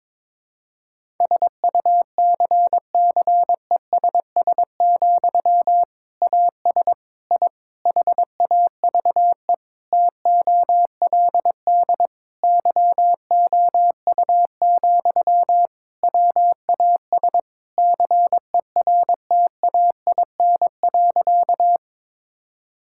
Morse code, heard as a keyed tone, about 22 wpm, 700 Hz. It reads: SUCCESS, AS I HAVE TOLD YOU, WAS CERTAIN.
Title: Sentences from Top 1000 Words - Review Part 6 - 22wpm